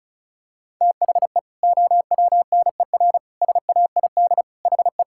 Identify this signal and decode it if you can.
Morse code, heard as a keyed tone, about 34 wpm, 700 Hz. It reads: THE OWNER SAID HE
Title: Sentences from Top 1600 Words 35wpm